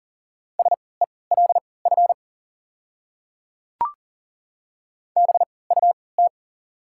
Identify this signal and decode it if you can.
Morse code, heard as a keyed tone, about 40 wpm, spaced as for 23 wpm, 700 Hz. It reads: SELF BUT
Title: 3x Character Spacing - Top 300 Words - 40wpm